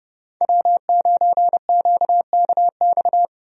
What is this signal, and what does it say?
Morse code, heard as a keyed tone, about 30 wpm, 700 Hz. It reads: W9QKX
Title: USA Call Signs - 5 Characters - Repeated 2x - 30wpm